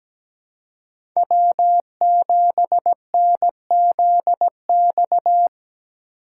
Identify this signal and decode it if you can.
Morse code, heard as a keyed tone, about 20 wpm, 700 Hz. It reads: W7NZX
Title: Speed Racing - USA Call Signs 17wpm